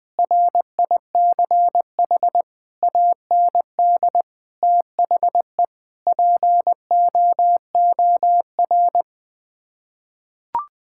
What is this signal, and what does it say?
Morse code, heard as a keyed tone, about 20 wpm, 700 Hz. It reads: RICH AND THE POOR
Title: Sets of 8 Words - 20wpm